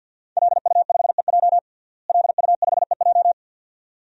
Morse code, heard as a keyed tone, about 52 wpm, 700 Hz. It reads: LU5EJ LU5EJ
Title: DXCC-100 Call Signs - 5 Characters - Repeated 2x - 50wpm